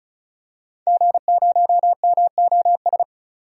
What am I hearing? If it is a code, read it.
Morse code, heard as a keyed tone, about 36 wpm, 700 Hz. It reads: G0MOS
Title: DXCC-100 Call Signs - 5 Characters - 35wpm